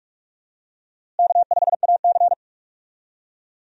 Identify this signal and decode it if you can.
Morse code, heard as a keyed tone, about 46 wpm, 700 Hz. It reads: K5AC
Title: USA Call Signs - 4 Characters - Repeated 2x - 45wpm